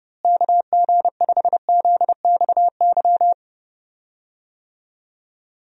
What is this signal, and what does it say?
Morse code, heard as a keyed tone, about 30 wpm, 700 Hz. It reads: KG5ZXY
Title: USA Call Signs - 6 Characters - Repeated 2x - 30wpm